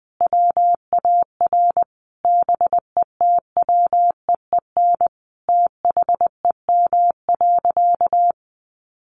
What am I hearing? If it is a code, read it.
Morse code, heard as a keyed tone, about 20 wpm, 700 Hz. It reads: WAR BETWEEN THEM.
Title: Sentences from Top 400 Words 20wpm